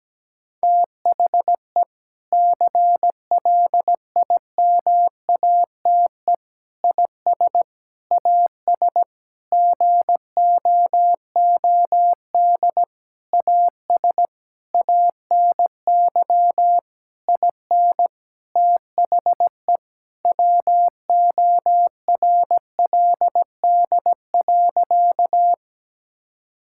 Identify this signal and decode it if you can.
Morse code, heard as a keyed tone, about 17 wpm, 700 Hz. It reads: THE CLIMATE IS AS GOOD AS ANY IN THE WORLD.
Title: Sentences from Top 1600 Words 17wpm